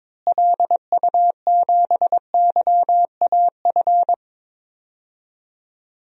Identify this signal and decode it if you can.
Morse code, heard as a keyed tone, about 22 wpm, 700 Hz. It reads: LU7YAF
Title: DXCC-100 Call Signs - 6 Characters - Repeated 3x - 22wpm